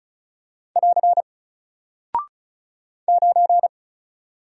Morse code, heard as a keyed tone, about 36 wpm, 700 Hz. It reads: <AR> 9
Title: Single Character (Letter:Number:Punctuation:Prosign) Alt 35wpm